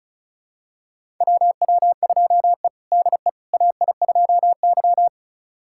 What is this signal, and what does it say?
Morse code, heard as a keyed tone, about 36 wpm, 700 Hz. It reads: WW2E DE AI2Y
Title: USA Callsign DE Callsign - 4 Characters - 35wpm